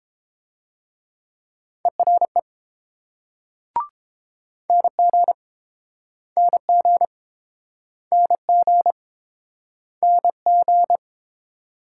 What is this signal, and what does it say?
Morse code, heard as a keyed tone, about 28 wpm, 700 Hz. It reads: ERE NG NG NG NG